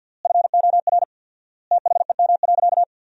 Morse code, heard as a keyed tone, about 50 wpm, 700 Hz. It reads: FOR THEM.
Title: 4x Word-Spacing - Sentences from Top 300 Words - 50wpm